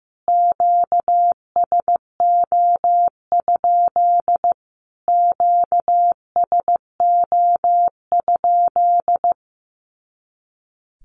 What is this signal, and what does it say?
Morse code, heard as a keyed tone, about 15 wpm, 700 Hz. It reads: QSO? QSO?
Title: Q-Codes Repeated 3x 15wpm